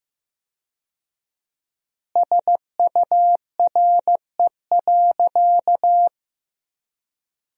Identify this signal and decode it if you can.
Morse code, heard as a keyed tone, about 15 wpm, 700 Hz. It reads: SURE.
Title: 4x Word-Spacing - Sentences from Top 500 Words - 15wpm